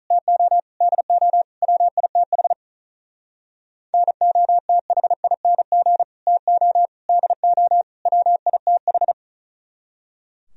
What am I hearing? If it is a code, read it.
Morse code, heard as a keyed tone, about 45 wpm, 700 Hz. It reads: TO DO WITH NOTHING TO DO WITH